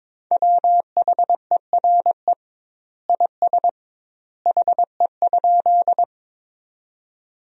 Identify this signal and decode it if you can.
Morse code, heard as a keyed tone, about 21 wpm, 700 Hz. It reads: WHERE IS HE?
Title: 2x Word-Spacing - Sentences from Top 200 Words - 22wpm